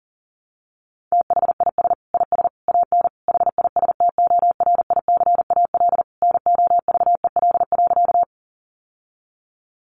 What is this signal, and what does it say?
Morse code, heard as a keyed tone, about 37 wpm, 700 Hz. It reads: THIS IS AN HISTORICAL NOVEL.
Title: Sentences from Top 1300 Words 40wpm